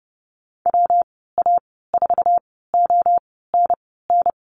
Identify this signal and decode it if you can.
Morse code, heard as a keyed tone, about 30 wpm, spaced as for 17 wpm, 700 Hz. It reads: WA4ONN
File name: Lesson 19g - Letter w - Call Signs 30wpm